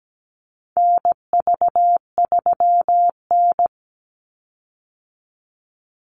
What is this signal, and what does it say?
Morse code, heard as a keyed tone, about 17 wpm, 700 Hz. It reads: NV3N
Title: Call Signs - 17wpm